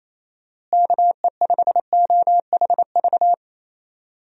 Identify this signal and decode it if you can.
Morse code, heard as a keyed tone, about 28 wpm, 700 Hz. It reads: KE5OHV
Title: USA Call Signs - 6 Characters - Repeated 3x - 28wpm